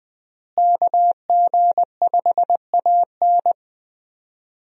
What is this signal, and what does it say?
Morse code, heard as a keyed tone, about 20 wpm, 700 Hz. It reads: KG5AN
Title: USA Call Signs - 5 Characters - Repeated 2x - 20wpm